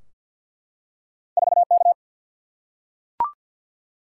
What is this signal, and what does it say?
Morse code, heard as a keyed tone, about 51 wpm, 700 Hz. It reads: 4 K
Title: Single-Letter Single-Number 50wpm